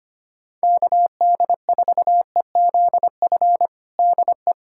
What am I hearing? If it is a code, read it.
Morse code, heard as a keyed tone, about 25 wpm, 700 Hz. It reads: KD4EZF DE
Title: USA Callsign DE Callsign 25wpm